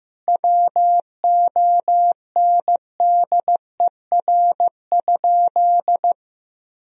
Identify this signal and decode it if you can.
Morse code, heard as a keyed tone, about 15 wpm, 700 Hz. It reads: WONDER?
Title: Sentences from Top 1000 Words - Review Part 3 - 15wpm